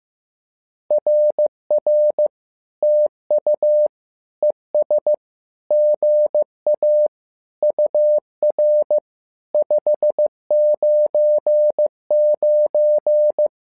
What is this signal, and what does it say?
Morse code, heard as a keyed tone, about 15 wpm, 600 Hz. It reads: RR TU ES GA UR 599
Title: SOTA QSO Practice 15wpm